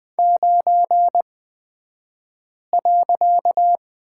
Morse code, heard as a keyed tone, about 20 wpm, spaced as for 5 wpm, 700 Hz. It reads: 9.
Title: 254 Lesson 41d - Period - Two Characters 20wpm